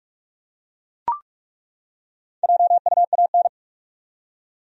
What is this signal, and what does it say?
Morse code, heard as a keyed tone, about 42 wpm, 700 Hz. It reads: JUAN
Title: USA Common Names - Max 4 Letters - 45wpm